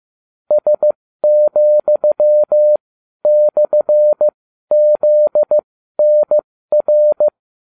Morse code, heard as a keyed tone, about 15 wpm, spaced as for 12 wpm, 600 Hz. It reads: S,/ZNR